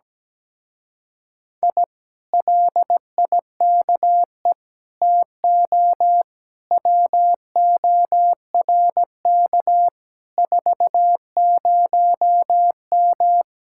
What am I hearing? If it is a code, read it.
Morse code, heard as a keyed tone, about 17 wpm, 700 Hz. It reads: I LIKE TO WORK 40M
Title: Speed-Racing Common QSO Phrases - 17wpm